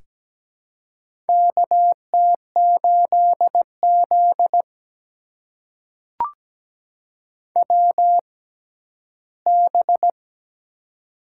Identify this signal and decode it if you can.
Morse code, heard as a keyed tone, about 17 wpm, spaced as for 12 wpm, 700 Hz. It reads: KT8Z WB
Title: USA Call Signs - 4 Characters - 6x Character Spacing - 17wpm